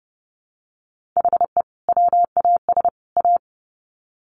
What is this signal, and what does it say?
Morse code, heard as a keyed tone, about 30 wpm, 700 Hz. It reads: HE WAS A
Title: Sets of 3 Words 30wpm